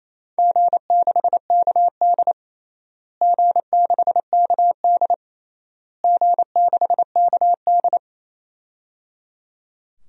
Morse code, heard as a keyed tone, about 27 wpm, 700 Hz. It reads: G6KD G6KD G6KD
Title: DXCC-100 Call Signs - 4 Characters - Repeated 3x - 28wpm